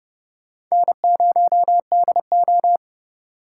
Morse code, heard as a keyed tone, about 40 wpm, 700 Hz. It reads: N0DO